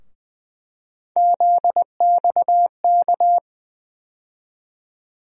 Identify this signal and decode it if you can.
Morse code, heard as a keyed tone, about 20 wpm, 700 Hz. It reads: ZXK
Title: USA Call Sign Postfixes - 3 Letters 20wpm